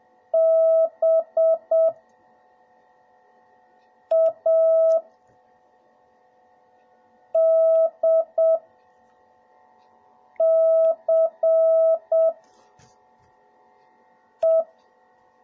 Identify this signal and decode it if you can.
Morse code, heard as a keyed tone, about 7 wpm, spaced as for 3 wpm, 650 Hz. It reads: BADCE